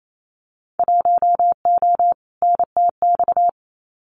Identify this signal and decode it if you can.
Morse code, heard as a keyed tone, about 28 wpm, 700 Hz. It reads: 1O NTX
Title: Winter Field Day Exchanges 28wpm